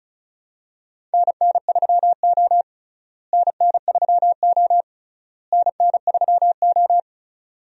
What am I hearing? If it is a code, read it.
Morse code, heard as a keyed tone, about 34 wpm, 700 Hz. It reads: NN3O NN3O NN3O
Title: USA Call Signs - 4 Characters - Repeated 3x - 35wpm